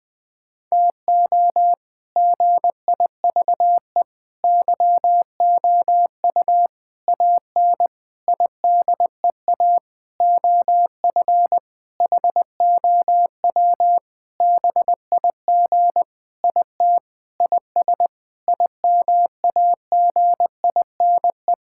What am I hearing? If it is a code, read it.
Morse code, heard as a keyed tone, about 21 wpm, 700 Hz. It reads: TO GIVE YOU AN IDEA OF HOW BIG IT IS IMAGINE
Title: Sentences with 7-Word Phrases - 20wpm